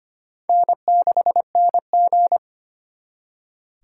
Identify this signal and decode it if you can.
Morse code, heard as a keyed tone, about 32 wpm, 700 Hz. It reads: N6NG